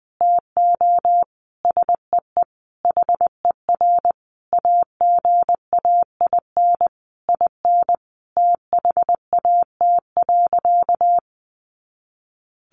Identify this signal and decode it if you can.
Morse code, heard as a keyed tone, about 20 wpm, 700 Hz. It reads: TO SEE HER AGAIN IN THAT.
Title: Sentences from Top 200 Words 20wpm